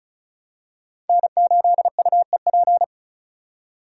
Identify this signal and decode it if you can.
Morse code, heard as a keyed tone, about 36 wpm, 690 Hz. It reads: N8UEP
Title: Code Groups - 5 Letter:Numbers - 35wpm